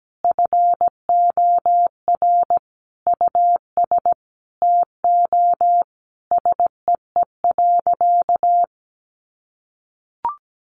Morse code, heard as a keyed tone, about 17 wpm, 700 Hz. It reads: FOR US TO SEE.
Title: Sentences from Top 500 Words 17wpm